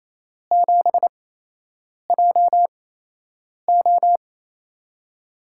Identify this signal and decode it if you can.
Morse code, heard as a keyed tone, about 28 wpm, spaced as for 8 wpm, 700 Hz. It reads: 7JO
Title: USA Call Signs - 8x Character Spacing - 28wpm